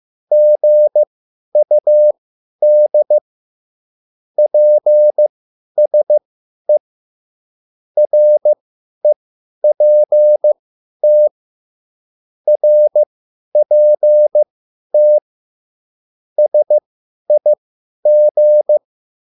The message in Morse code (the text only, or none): GUD PSE REPT RPT SIG